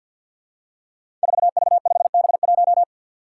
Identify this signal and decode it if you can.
Morse code, heard as a keyed tone, about 50 wpm, 700 Hz. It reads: 4V5B1
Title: Five Challenging Letter-Number Combinations 50wpm